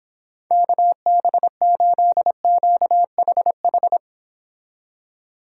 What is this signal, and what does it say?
Morse code, heard as a keyed tone, about 26 wpm, 700 Hz. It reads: KB8QHH